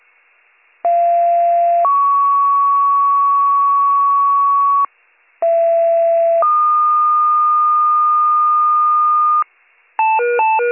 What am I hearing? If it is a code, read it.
Two-tone page, heard as a one-second tone, then a three-second tone, about 690 then 1120 Hz, about 670 then 1220 Hz. It is heard from 0.8 to 9.4 s.